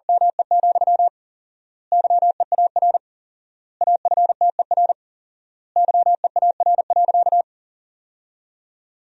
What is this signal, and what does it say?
Morse code, heard as a keyed tone, about 40 wpm, 700 Hz. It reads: ME, YEAR AFTER YEAR.
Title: 4x Word-Spacing - Sentences from Top 300 Words - 40wpm